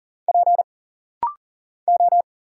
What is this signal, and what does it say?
Morse code, heard as a keyed tone, about 40 wpm, 700 Hz. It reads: P O
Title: Single Letters - Rapid-Fire 40wpm